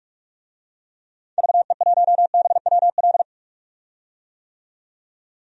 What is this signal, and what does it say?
Morse code, heard as a keyed tone, about 46 wpm, 700 Hz. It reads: VE1BWL
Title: DXCC-100 Call Signs - 6 Characters - 45wpm